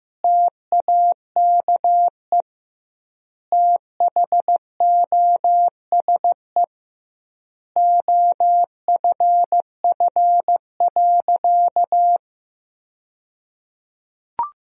Morse code, heard as a keyed tone, about 15 wpm, 700 Hz. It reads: TAKE THOSE OFF.
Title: 2x Word-Spacing - Sentences from Top 500 Words - 15wpm